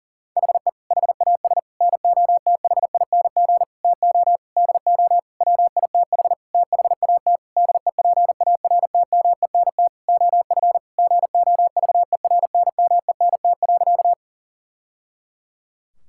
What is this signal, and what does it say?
Morse code, heard as a keyed tone, about 40 wpm, 700 Hz. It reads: HE HAS NOTHING TO DO WITH THAT DEPARTMENT OF GOVERNMENT.
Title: Sentences from Top 1000 Words - Review Part 5 - 40wpm